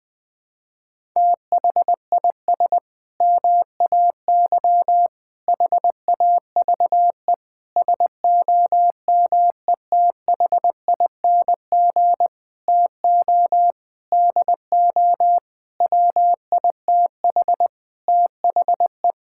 Morse code, heard as a keyed tone, about 21 wpm, 700 Hz. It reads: THIS MAY HAVE SOMETHING TO DO WITH THE
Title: Sentences with 8-Word Phrases - Spoken Beginning - 20wpm